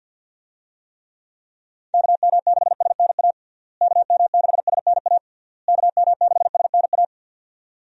Morse code, heard as a keyed tone, about 51 wpm, 700 Hz. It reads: KM6SNA KM6SNA KM6SNA
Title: USA Call Signs - 6 Characters - Repeated 3x - 50wpm